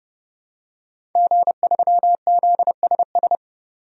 Morse code, heard as a keyed tone, about 30 wpm, 700 Hz. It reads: G3ZSS
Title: DXCC-100 Call Signs - 5 Characters - 30wpm